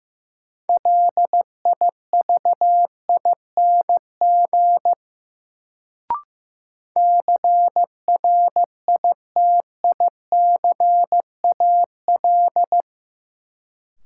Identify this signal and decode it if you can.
Morse code, heard as a keyed tone, about 15 wpm, 700 Hz. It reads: LIVING CRITICAL
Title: Sentences from Top 1100 Words - New Word Drill Alt - 15wpm